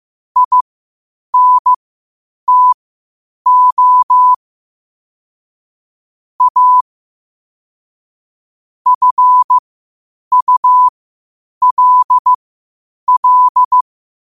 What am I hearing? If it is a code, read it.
Morse code, heard as a keyed tone, about 15 wpm, spaced as for 8 wpm, 1000 Hz. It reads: INTO A FULL
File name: qotdfiles / 8wpm